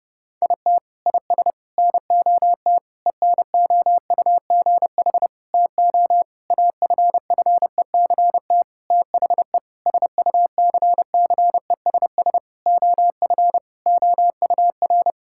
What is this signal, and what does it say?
Morse code, heard as a keyed tone, about 32 wpm, 700 Hz. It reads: IT IS NOT ENOUGH TO AFFECT THE SUCCESS OF OUR
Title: Sentences from Top 1000 Words - Review Part 6 - 30wpm